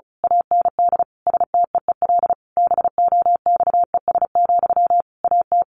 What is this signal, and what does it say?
Morse code, heard as a keyed tone, about 34 wpm, 700 Hz. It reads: AND STEEL BOXES, AT
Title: A Sound Of Thunder 35wpm